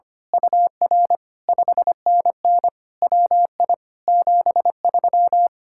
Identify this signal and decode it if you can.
Morse code, heard as a keyed tone, about 24 wpm, 700 Hz. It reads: UR 5NN WI 73
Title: Speed-Racing Common QSO Phrases - 25wpm